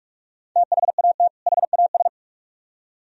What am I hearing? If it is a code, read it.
Morse code, heard as a keyed tone, about 41 wpm, 700 Hz. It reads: THAT HAS